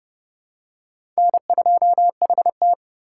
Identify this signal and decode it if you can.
Morse code, heard as a keyed tone, about 30 wpm, 700 Hz. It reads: N2HT
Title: USA Call Signs - 4 Characters - 2x Character Spacing - 30wpm